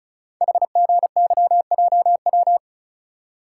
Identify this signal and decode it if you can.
Morse code, heard as a keyed tone, about 35 wpm, 690 Hz. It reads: HGYJW
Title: Code Groups - 5 Letters - 35wpm